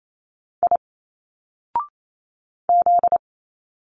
Morse code, heard as a keyed tone, about 28 wpm, 700 Hz. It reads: I Z
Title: Single Letters 28wpm